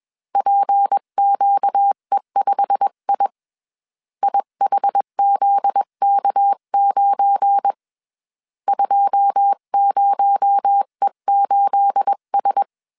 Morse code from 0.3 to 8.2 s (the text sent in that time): PQE5I IHZK9